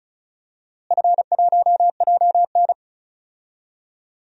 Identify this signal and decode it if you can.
Morse code, heard as a keyed tone, about 36 wpm, 700 Hz. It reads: F1JN
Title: DXCC-100 Call Signs - 4 Characters - Repeated 2x - 35wpm